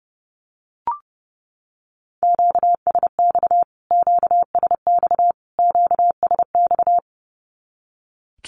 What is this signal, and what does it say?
Morse code, heard as a keyed tone, about 30 wpm, 700 Hz. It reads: QSX QSX QSX
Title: Q-Codes Repeated 3x 30wpm